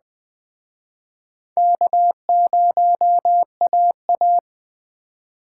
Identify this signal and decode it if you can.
Morse code, heard as a keyed tone, about 21 wpm, 700 Hz. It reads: K0AA